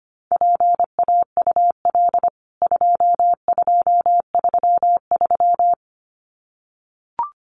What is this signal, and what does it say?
Morse code, heard as a keyed tone, about 25 wpm, 700 Hz. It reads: PAUL 2233
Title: CWT Exchange 25wpm